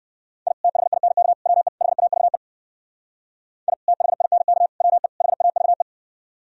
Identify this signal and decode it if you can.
Morse code, heard as a keyed tone, about 59 wpm, 700 Hz. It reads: I THINK WE HAVE I THINK WE HAVE